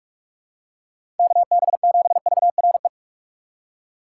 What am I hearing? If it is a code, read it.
Morse code, heard as a keyed tone, about 42 wpm, 700 Hz. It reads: KB7VRE
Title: USA Call Signs - 6 Characters - 4x Character Spacing - 45wpm